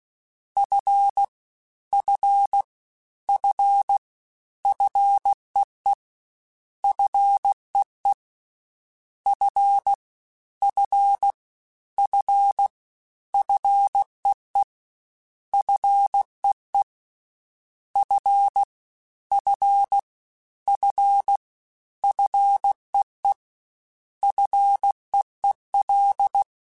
Morse code, heard as a keyed tone, about 16 wpm, 800 Hz. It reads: F F F FEE FEE F F F FEE FEE F F F FEE FEEL
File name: F and L / F and L mp3 / F and L Prefixes 3x2 Twice @15